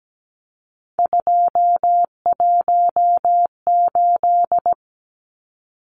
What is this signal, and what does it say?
Morse code, heard as a keyed tone, about 17 wpm, 700 Hz. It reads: 218
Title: Numbers - 3-Digit 17wpm